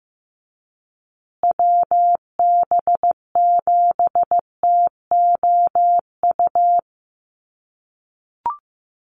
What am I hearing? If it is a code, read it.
Morse code, heard as a keyed tone, about 15 wpm, 700 Hz. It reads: WB7TOU
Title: USA Call Signs Repeated 2x 15wpm